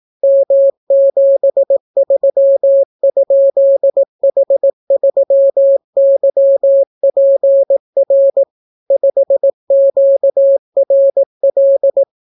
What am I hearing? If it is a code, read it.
Morse code, heard as a keyed tone, about 18 wpm, 550 Hz. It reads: M73?H3YPR 5QRL